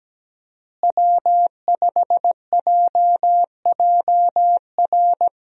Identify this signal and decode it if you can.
Morse code, heard as a keyed tone, about 17 wpm, 700 Hz. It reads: W5JJR
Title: USA Call Signs - 5 Characters - Repeated 3x - 17wpm